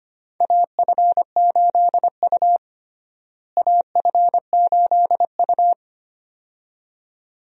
Morse code, heard as a keyed tone, about 25 wpm, 700 Hz. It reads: AF8U AF8U
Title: USA Call Signs - 4 Characters - Repeated 2x - 25wpm